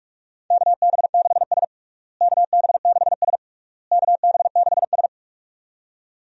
Morse code, heard as a keyed tone, about 47 wpm, 700 Hz. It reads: KB6S KB6S KB6S
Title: USA Call Signs - 4 Characters - Repeated 3x - 45wpm